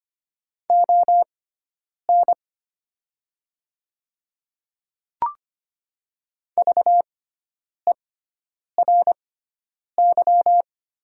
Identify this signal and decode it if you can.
Morse code, heard as a keyed tone, about 25 wpm, spaced as for 9 wpm, 700 Hz. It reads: ON VERY
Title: 6x Character Spacing - Top 200 Words - 25wpm